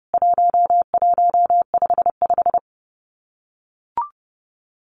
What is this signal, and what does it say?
Morse code, heard as a keyed tone, about 28 wpm, 700 Hz. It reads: 1155
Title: CWT Exchange 30wpm